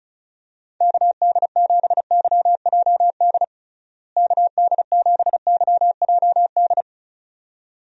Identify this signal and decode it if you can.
Morse code, heard as a keyed tone, about 36 wpm, 700 Hz. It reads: KD7YJD KD7YJD
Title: USA Call Signs - 6 Characters - Repeated 2x - 35wpm